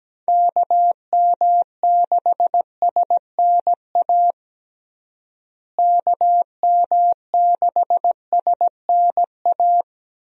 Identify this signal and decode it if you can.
Morse code, heard as a keyed tone, about 17 wpm, 700 Hz. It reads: KM6SNA KM6SNA
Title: USA Call Signs - 6 Characters - Repeated 3x - 17wpm